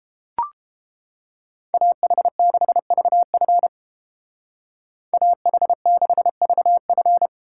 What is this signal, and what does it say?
Morse code, heard as a keyed tone, about 28 wpm, 700 Hz. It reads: AH6VF AH6VF